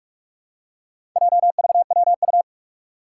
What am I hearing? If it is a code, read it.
Morse code, heard as a keyed tone, about 42 wpm, 700 Hz. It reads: JVWU
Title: Four Challenging Letter-Number Combinations 45wpm